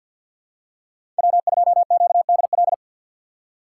Morse code, heard as a keyed tone, about 53 wpm, 700 Hz. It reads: W2QDL
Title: USA Call Signs - 5 Characters - 3x Character Spacing - 50wpm